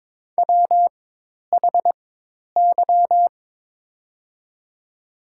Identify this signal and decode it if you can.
Morse code, heard as a keyed tone, about 22 wpm, spaced as for 10 wpm, 700 Hz. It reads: WHY
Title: 4x Character Spacing - Top 200 Words - 22wpm